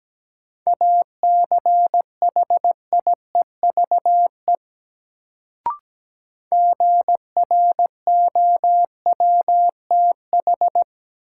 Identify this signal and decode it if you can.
Morse code, heard as a keyed tone, about 17 wpm, 700 Hz. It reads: ACHIEVE GROWTH
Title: Sentences from Top 1000 Words - Review Part 5 - New Word Drill - 17wpm